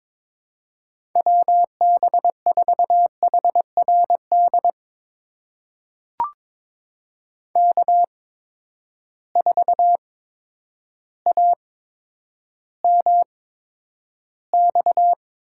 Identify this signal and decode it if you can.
Morse code, heard as a keyed tone, about 22 wpm, spaced as for 10 wpm, 700 Hz. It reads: WB4HRD K4AMX
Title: USA Call Signs - 8x Character Spacing - 22wpm